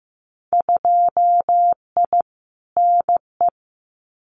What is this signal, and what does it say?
Morse code, heard as a keyed tone, about 15 wpm, 700 Hz. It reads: 2I NE
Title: Winter Field Day Exchanges Encore 15wpm